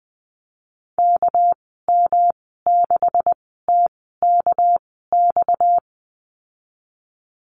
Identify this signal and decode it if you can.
Morse code, heard as a keyed tone, about 20 wpm, spaced as for 14 wpm, 700 Hz. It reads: KM6TKX
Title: 245 Lesson 39g - Letter z - Call Signs 20wpm